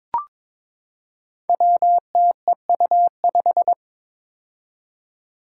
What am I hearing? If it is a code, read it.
Morse code, heard as a keyed tone, about 22 wpm, 700 Hz. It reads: WTEU5
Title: Five Challenging Letter-Number Combinations 22wpm